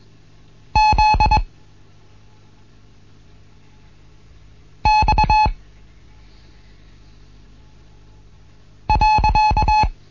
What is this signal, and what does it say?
Morse code, heard as a keyed tone, about 21 wpm, 860 Hz. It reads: Z X .
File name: Morsen / Kurs A 15